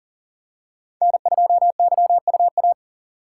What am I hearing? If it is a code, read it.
Morse code, heard as a keyed tone, about 40 wpm, 700 Hz. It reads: N2YUA
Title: USA Call Signs - 5 Characters - 2x Character Spacing - 40wpm